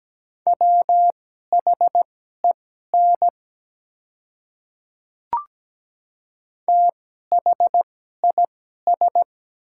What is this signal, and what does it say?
Morse code, heard as a keyed tone, about 17 wpm, spaced as for 12 wpm, 700 Hz. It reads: WHEN THIS
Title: 2x Character Spacing - Top 100 Words - 17wpm